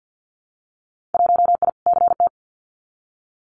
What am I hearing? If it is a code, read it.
Morse code, heard as a keyed tone, about 50 wpm, 700 Hz. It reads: 1I CT
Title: Winter Field Day Exchanges 50wpm